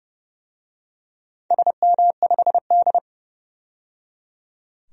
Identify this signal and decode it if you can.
Morse code, heard as a keyed tone, about 29 wpm, 700 Hz. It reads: SM5D
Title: DXCC-100 Call Signs - 4 Characters - 30wpm